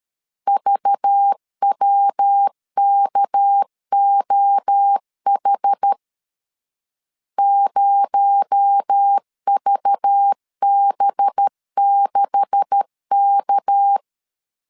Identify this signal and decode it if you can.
Morse code, heard as a keyed tone, about 13 wpm, 800 Hz. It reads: VWKOH 0VB6K
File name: CW / DET / 1 9 1500C S C 12wpm 3811